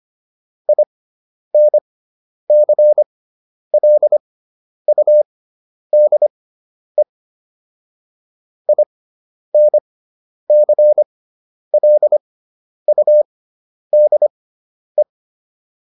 Morse code, heard as a keyed tone, about 25 wpm, spaced as for 10 wpm, 600 Hz. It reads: INCLUDE INCLUDE